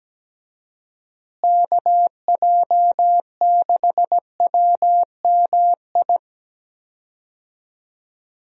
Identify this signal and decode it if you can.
Morse code, heard as a keyed tone, about 17 wpm, 700 Hz. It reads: KJ6WMI
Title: USA Call Signs - 6 Characters - Repeated 3x - 17wpm